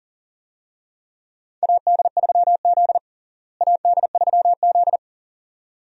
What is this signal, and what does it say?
Morse code, heard as a keyed tone, about 40 wpm, 700 Hz. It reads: AD3Z AD3Z
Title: USA Call Signs - 4 Characters - Repeated 2x - 40wpm